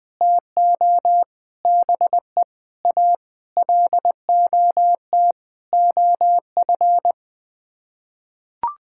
Morse code, heard as a keyed tone, about 21 wpm, 700 Hz. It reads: TO BE A LOT OF
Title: Sets of 8 Words - 20wpm